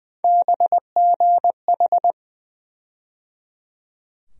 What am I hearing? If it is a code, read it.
Morse code, heard as a keyed tone, about 20 wpm, 700 Hz. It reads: BGH
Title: USA Call Sign Postfixes - 3 Letters 20wpm